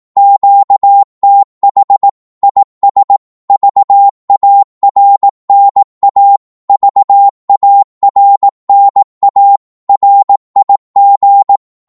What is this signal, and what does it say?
Morse code, heard as a keyed tone, about 18 wpm, 800 Hz. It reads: QTH IS VARNA VARNA RIG
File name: Morse / qso-18wpm